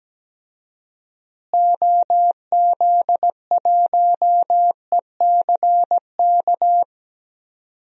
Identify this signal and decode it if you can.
Morse code, heard as a keyed tone, about 17 wpm, 700 Hz. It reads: OZ1ECK
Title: DXCC-100 Call Signs - 6 Characters - Repeated 3x - 17wpm